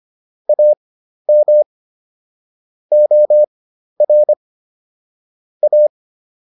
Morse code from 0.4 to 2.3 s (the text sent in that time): AM